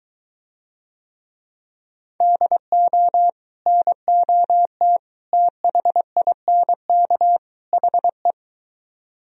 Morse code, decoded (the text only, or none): DO NOT THINK HE